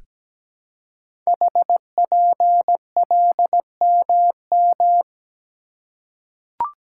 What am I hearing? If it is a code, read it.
Morse code, heard as a keyed tone, about 17 wpm, 700 Hz. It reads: HPLMM
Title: Code Groups - 5 Letters - 17wpm